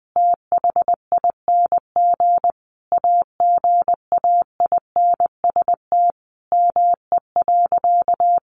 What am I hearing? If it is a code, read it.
Morse code, heard as a keyed tone, about 20 wpm, 700 Hz. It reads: THING AGAINST ME.
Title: Sentences from Top 1100 Words 20wpm